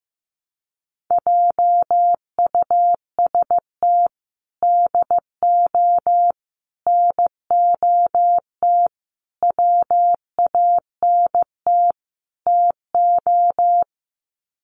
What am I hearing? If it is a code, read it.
Morse code, heard as a keyed tone, about 15 wpm, 700 Hz. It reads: JUST DO NOT WANT TO
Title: Sets of 5 Words Encore 15wpm